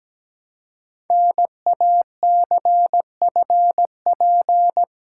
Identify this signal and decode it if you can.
Morse code, heard as a keyed tone, about 17 wpm, 700 Hz. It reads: NACFP
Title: Code Groups - 5 Letter:Numbers - 17wpm